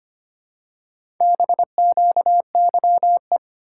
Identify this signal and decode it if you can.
Morse code, heard as a keyed tone, about 25 wpm, 700 Hz. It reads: BQYE
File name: practice / 25wpm / Four Letter-Number Combinations - 25wpm